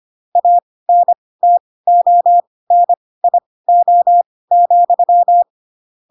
Morse code, heard as a keyed tone, about 25 wpm, spaced as for 18 wpm, 700 Hz. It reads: ANTONIO,
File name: Pinocchio / Pinocchio ch1 12 25 ik1hge